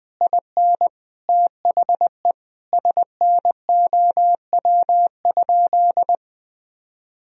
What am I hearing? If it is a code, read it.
Morse code, heard as a keyed tone, about 20 wpm, 700 Hz. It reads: IN THE SNOW?
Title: Sentences from Top 1000 Words - Review Part 3 - 20wpm